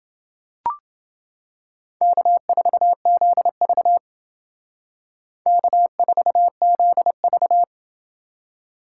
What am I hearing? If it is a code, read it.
Morse code, heard as a keyed tone, about 29 wpm, 700 Hz. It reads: K4ZV K4ZV